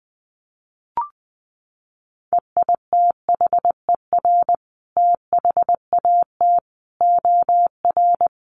Morse code, heard as a keyed tone, about 20 wpm, 700 Hz. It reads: EITHER THAT OR
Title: Sentences from Top 1400 Words 20wpm